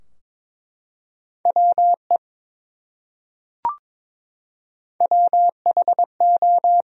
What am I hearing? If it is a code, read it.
Morse code, heard as a keyed tone, about 22 wpm, 700 Hz. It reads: WE WHO
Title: Top 100 Words Repeated 22wpm